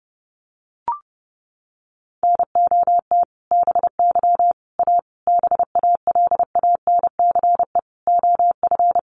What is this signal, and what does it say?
Morse code, heard as a keyed tone, about 30 wpm, 700 Hz. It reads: NOT BY A BALANCE OF
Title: Sentences from Top 900 Words 30wpm